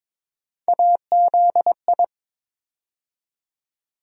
Morse code, heard as a keyed tone, about 22 wpm, 700 Hz. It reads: AZI
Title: USA Call Sign Postfixes - 3 Letters 22wpm